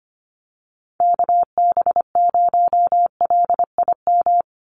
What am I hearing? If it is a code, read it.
Morse code, heard as a keyed tone, about 25 wpm, 700 Hz. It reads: KB0LIM
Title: USA Call Signs 25wpm